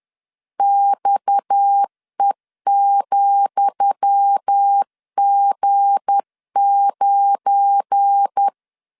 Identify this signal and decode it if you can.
Morse code, heard as a keyed tone, about 10 wpm, 800 Hz. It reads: XE,G9
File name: LEZIONI / 10 SIGNS / 10D COMA 10wpm 3013